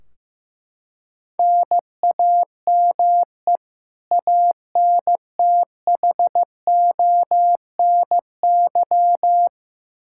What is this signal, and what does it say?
Morse code, heard as a keyed tone, about 15 wpm, 700 Hz. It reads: NAME ANTHONY
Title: K1USN SST QSO Practice 15wpm